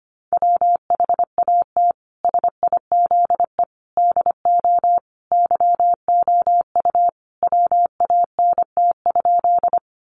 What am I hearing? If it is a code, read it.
Morse code, heard as a keyed tone, about 25 wpm, 700 Hz. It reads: WHAT SIZE DO YOU WANT?
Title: Sentences from Top 400 Words 25wpm